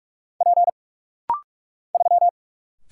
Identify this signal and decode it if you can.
Morse code, heard as a keyed tone, about 45 wpm, 700 Hz. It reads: P 3
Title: Single Letter-Number - Rapid-Fire 45wpm